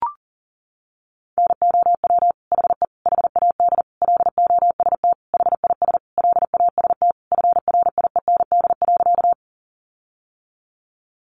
Morse code, heard as a keyed tone, about 40 wpm, 700 Hz. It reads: NOW HE HAD LOST HIS LAST FRIEND.
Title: Sentences from Top 1400 Words 40wpm